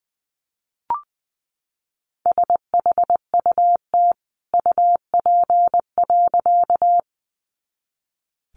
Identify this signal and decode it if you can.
Morse code, heard as a keyed tone, about 20 wpm, 700 Hz. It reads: SHUT UP.
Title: A Sound Of Thunder 20wpm